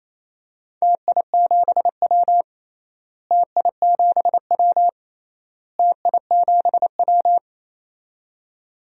Morse code, heard as a keyed tone, about 28 wpm, 700 Hz. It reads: TI7W TI7W TI7W
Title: DXCC-100 Call Signs - 4 Characters - Repeated 3x - 28wpm